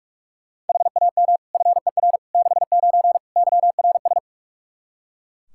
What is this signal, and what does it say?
Morse code, heard as a keyed tone, about 41 wpm, 700 Hz. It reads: HAM FER 69 YRS
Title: K1USN SST QSO Practice 45wpm